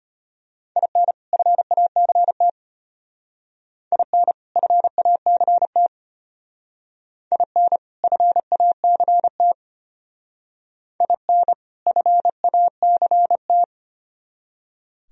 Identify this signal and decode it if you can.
Morse code, heard as a keyed tone, about 32 wpm, 700 Hz. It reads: IN FACT IN FACT IN FACT IN FACT